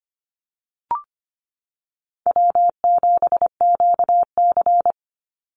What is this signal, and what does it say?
Morse code, heard as a keyed tone, about 25 wpm, 700 Hz. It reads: W7QC
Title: Call Signs Repeated 3x - 25wpm